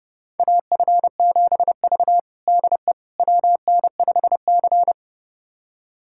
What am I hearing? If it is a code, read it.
Morse code, heard as a keyed tone, about 30 wpm, 700 Hz. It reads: AF7V DE WN5C
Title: USA Callsign DE Callsign - 4 Characters - 30wpm